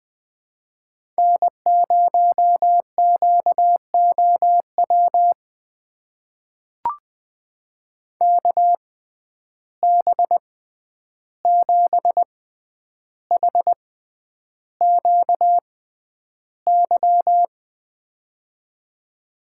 Morse code, heard as a keyed tone, about 20 wpm, spaced as for 10 wpm, 700 Hz. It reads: N0QOW KB7HQY
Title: USA Call Signs - 6x Character Spacing - 20wpm